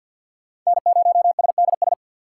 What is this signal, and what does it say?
Morse code, heard as a keyed tone, about 49 wpm, 690 Hz. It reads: N0SDS
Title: USA Call Signs - 5 Characters - 2x Character Spacing - 50wpm